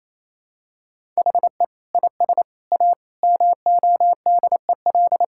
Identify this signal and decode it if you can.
Morse code, heard as a keyed tone, about 28 wpm, 700 Hz. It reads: HE IS A MODEL
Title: Sentences from Top 1600 Words 28wpm